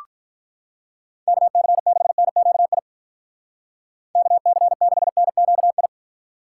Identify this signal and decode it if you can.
Morse code, heard as a keyed tone, about 43 wpm, 700 Hz. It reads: KC6NQI KC6NQI
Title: Speed Racing - USA 2x3 Call Signs 35wpm